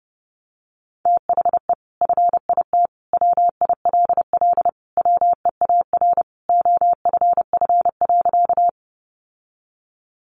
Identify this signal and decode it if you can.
Morse code, heard as a keyed tone, about 30 wpm, 700 Hz. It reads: THE FIT WILL WEAR OFF.
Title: Sentences from Top 1500 Words 30wpm